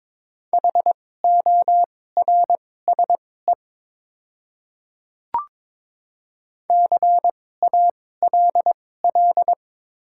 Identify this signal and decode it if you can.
Morse code, heard as a keyed tone, about 23 wpm, spaced as for 16 wpm, 700 Hz. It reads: HORSE CALL
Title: 2x Character Spacing - Top 300 Words - 22wpm